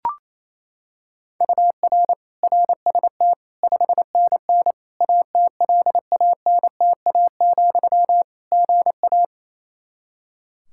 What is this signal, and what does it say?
Morse code, heard as a keyed tone, about 28 wpm, 700 Hz. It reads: UR RST 5NN ATLANTA, GA
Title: Practice QSO - Exchange 1 - Beginning - 28wpm